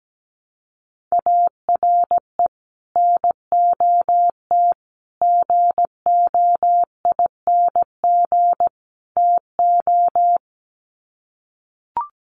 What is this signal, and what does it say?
Morse code, heard as a keyed tone, about 17 wpm, 700 Hz. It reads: ARE NOT GOING TO
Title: Sets of 4 Words 17wpm